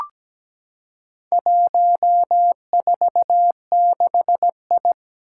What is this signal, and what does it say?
Morse code, heard as a keyed tone, about 17 wpm, 700 Hz. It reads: 146I
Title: Four Challenging Letter-Number Combinations 17wpm